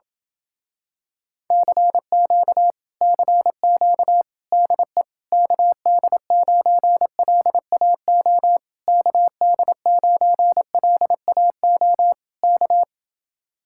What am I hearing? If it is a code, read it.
Morse code, heard as a keyed tone, about 28 wpm, 700 Hz. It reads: CQ CQ DE KD9LAO KD9LAO K